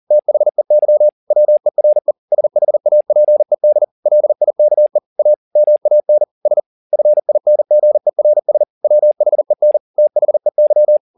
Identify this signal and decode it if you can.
Morse code, heard as a keyed tone, about 40 wpm, 600 Hz. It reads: THEY WERE SHAPED LIKE A MAN S FINGERS WHEN THEY
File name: the old man and the sea 2 40wpm